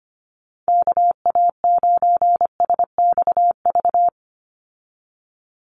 Morse code, heard as a keyed tone, about 25 wpm, 700 Hz. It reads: KA9SXV
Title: Call Signs Repeated 3x - 25wpm